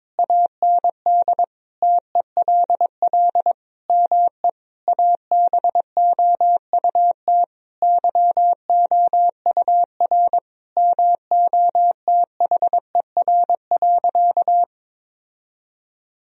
Morse code, heard as a keyed tone, about 22 wpm, 700 Hz. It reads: AND TELL ME ABOUT YOUR MOTHER.
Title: Sentences from Top 1000 Words - Review Part 3 - 22wpm